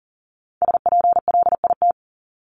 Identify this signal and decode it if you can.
Morse code, heard as a keyed tone, about 40 wpm, 700 Hz. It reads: SPLIT
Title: Top 1751-2000 Words 40wpm